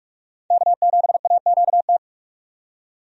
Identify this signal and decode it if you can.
Morse code, heard as a keyed tone, about 42 wpm, 700 Hz. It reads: K7AQT
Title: USA Call Signs - 5 Characters - 45wpm